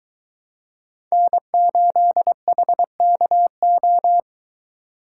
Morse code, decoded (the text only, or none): N8HKO